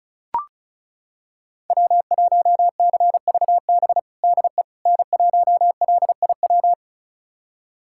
Morse code, heard as a keyed tone, about 34 wpm, 700 Hz. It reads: W1CVB DE N1LIW
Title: USA Callsign DE Callsign - 5 Characters - 35wpm